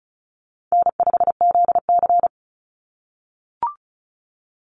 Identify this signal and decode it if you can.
Morse code, heard as a keyed tone, about 36 wpm, 700 Hz. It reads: N5ZC
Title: NA SOTA Chaser Callsigns 35wpm